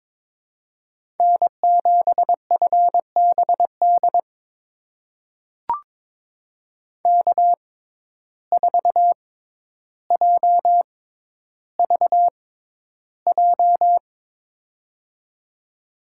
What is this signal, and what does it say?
Morse code, heard as a keyed tone, about 22 wpm, spaced as for 11 wpm, 700 Hz. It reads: N7FBD K4JVJ
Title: USA Call Signs - 6x Character Spacing - 22wpm